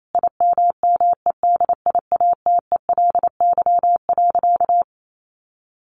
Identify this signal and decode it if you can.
Morse code, heard as a keyed tone, about 28 wpm, 700 Hz. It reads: IMMEDIATELY.
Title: Sentences from Top 1100 Words 28wpm